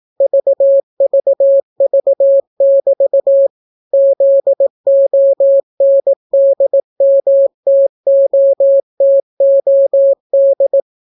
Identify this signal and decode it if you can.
Morse code, heard as a keyed tone, about 18 wpm, 550 Hz. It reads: VVV<BT> ZONDMTOTOD